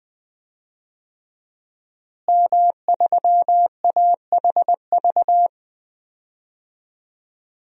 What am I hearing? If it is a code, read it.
Morse code, heard as a keyed tone, about 21 wpm, 700 Hz. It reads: M3AHV
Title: DXCC-100 Call Signs - 5 Characters - Repeated 2x - 20wpm